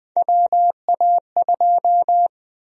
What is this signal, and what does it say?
Morse code, heard as a keyed tone, about 20 wpm, 700 Hz. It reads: WA2
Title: USA Call Sign Prefixes - Repeated 2x - 20wpm